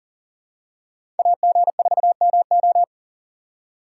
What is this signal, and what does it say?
Morse code, heard as a keyed tone, about 39 wpm, 700 Hz. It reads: AG4MO
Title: USA Call Signs - 5 Characters - 4x Character Spacing - 40wpm